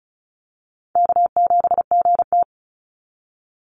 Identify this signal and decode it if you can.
Morse code, heard as a keyed tone, about 36 wpm, 700 Hz. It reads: K7GT
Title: NA SOTA Chaser Callsigns 35wpm